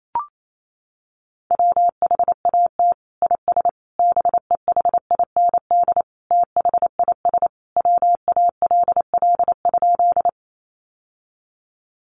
Morse code, heard as a keyed tone, about 28 wpm, 700 Hz. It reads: WHAT IS BEHIND THIS WALL?
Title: Sentences from Top 1100 Words 28wpm